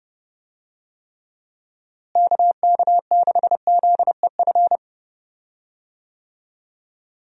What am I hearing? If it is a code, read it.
Morse code, heard as a keyed tone, about 31 wpm, 700 Hz. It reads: KK6ZEF
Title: USA Call Signs - 6 Characters - Repeated 3x - 30wpm